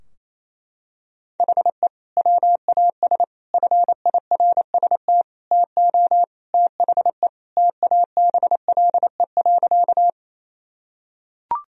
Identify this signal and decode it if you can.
Morse code, heard as a keyed tone, about 28 wpm, 700 Hz. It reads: HE WAS FIRST TO THE TABLE.
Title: Sentences from Top 1000 Words - Review Part 2 - 28wpm